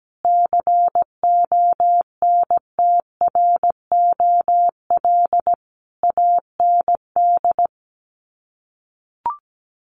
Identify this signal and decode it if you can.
Morse code, heard as a keyed tone, about 17 wpm, 700 Hz. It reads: CONTROL AND
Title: Sets of 5 Words Encore2 17wpm